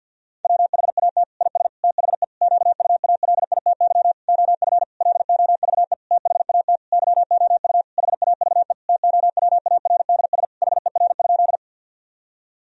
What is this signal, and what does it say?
Morse code, heard as a keyed tone, about 50 wpm, 700 Hz. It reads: WHAT IS THE QUALITY OF LOVE THAT YOU HAVE TOWARDS HER?
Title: Sentences from Top 1000 Words - Review Part 4 - 50wpm